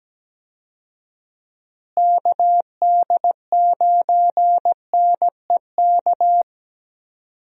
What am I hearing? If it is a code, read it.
Morse code, heard as a keyed tone, about 17 wpm, 700 Hz. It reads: KD9NEK
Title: USA Call Signs - 6 Characters - Repeated 3x - 17wpm